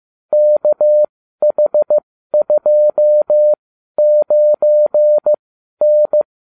Morse code, heard as a keyed tone, about 15 wpm, spaced as for 12 wpm, 600 Hz. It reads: KH29N